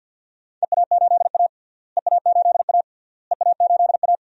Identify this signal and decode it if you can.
Morse code, heard as a keyed tone, about 53 wpm, 700 Hz. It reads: EA8A EA8A EA8A
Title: DXCC-100 Call Signs - 4 Characters - Repeated 3x - 50wpm